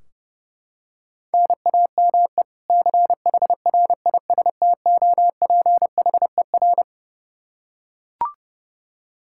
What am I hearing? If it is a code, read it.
Morse code, heard as a keyed tone, about 32 wpm, 700 Hz. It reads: NAME CHRISTOPHER
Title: Practice QSO - Exchange 1 - Beginning - 30wpm